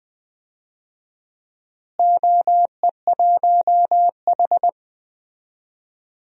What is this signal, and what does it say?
Morse code, heard as a keyed tone, about 20 wpm, 700 Hz. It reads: OE1H
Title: DXCC-100 Call Signs - 4 Characters - Repeated 3x - 20wpm